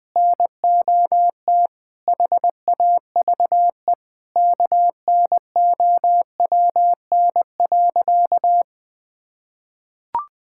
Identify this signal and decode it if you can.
Morse code, heard as a keyed tone, about 21 wpm, 700 Hz. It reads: NOT HAVE KNOWN.
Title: Sentences with 3-Word Phrases - 20wpm